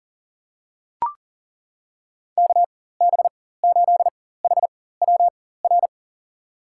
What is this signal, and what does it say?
Morse code, heard as a keyed tone, about 40 wpm, spaced as for 19 wpm, 700 Hz. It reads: KB8HWR
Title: USA Call Signs - 6 Characters - 4x Character Spacing - 40wpm